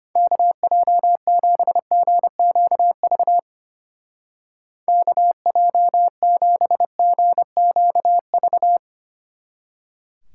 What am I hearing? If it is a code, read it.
Morse code, heard as a keyed tone, about 32 wpm, 700 Hz. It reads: KJ7GQV KJ7GQV